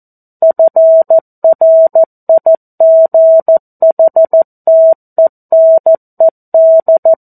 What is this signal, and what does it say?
Morse code, heard as a keyed tone, about 14 wpm, 650 Hz. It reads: FRIGHTENED
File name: 02-Chapter-02 14 WPM